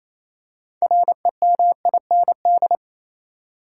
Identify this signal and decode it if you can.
Morse code, heard as a keyed tone, about 27 wpm, 700 Hz. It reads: REMIND
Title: Six Letter Words - 28wpm